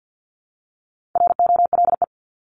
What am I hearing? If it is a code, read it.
Morse code, heard as a keyed tone, about 50 wpm, 700 Hz. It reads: ROLE
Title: Top 751-1000 Words 50wpm